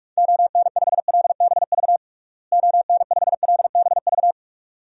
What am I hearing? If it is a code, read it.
Morse code, heard as a keyed tone, about 42 wpm, 700 Hz. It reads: ON5LBV ON5LBV
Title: DXCC-100 Call Signs - 6 Characters - Repeated 2x - 45wpm